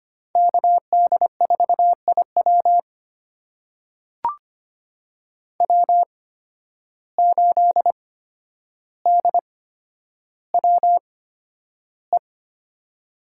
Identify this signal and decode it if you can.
Morse code, heard as a keyed tone, about 25 wpm, spaced as for 11 wpm, 700 Hz. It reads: KD4IW W8DWE
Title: USA Call Signs - 5 Characters - 8x Character Spacing - 25wpm